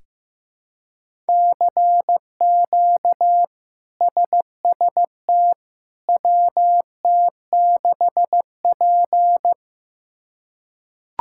Morse code, heard as a keyed tone, about 15 wpm, 700 Hz. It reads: CQ SST WT6P
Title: K1USN SST QSO Practice - 15wpm